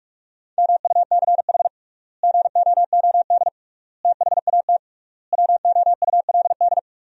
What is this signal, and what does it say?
Morse code, heard as a keyed tone, about 46 wpm, 700 Hz. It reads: MUCH GOOD THAT WOULD
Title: 3x Word-Spacing - Sentences from Top 200 Words - 45wpm